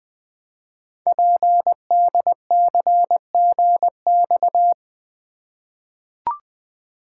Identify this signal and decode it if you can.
Morse code, heard as a keyed tone, about 20 wpm, 700 Hz. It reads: PDCGX
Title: Code Groups - 5 Letters - 20wpm